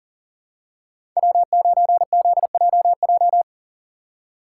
Morse code, heard as a keyed tone, about 39 wpm, 700 Hz. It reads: W9ZJJ
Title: USA Call Signs - 5 Characters - 6x Character Spacing - 40wpm